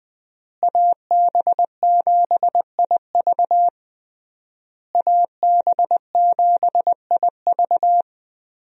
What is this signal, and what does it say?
Morse code, heard as a keyed tone, about 20 wpm, 700 Hz. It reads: AB7IV AB7IV
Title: USA Call Signs - 5 Characters - Repeated 3x - 20wpm